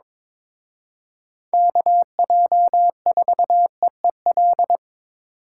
Morse code, heard as a keyed tone, about 27 wpm, 700 Hz. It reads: KJ4EEL
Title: Speed Racing - USA 2x3 Call Signs 22wpm